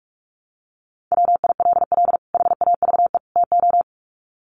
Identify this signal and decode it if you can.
Morse code, heard as a keyed tone, about 44 wpm, 700 Hz. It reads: WILL HAVE TO
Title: Sets of 3 Words 45wpm